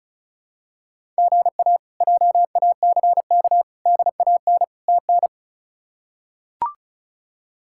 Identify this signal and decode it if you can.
Morse code, heard as a keyed tone, about 36 wpm, 700 Hz. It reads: GA JACK DAN TN
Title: K1USN SST QSO Practice - 35wpm